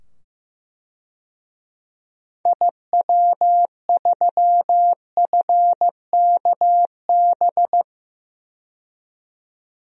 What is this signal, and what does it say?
Morse code, heard as a keyed tone, about 15 wpm, 700 Hz. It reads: IW3FKB
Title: DXCC-100 Call Signs - 6 Characters - 15wpm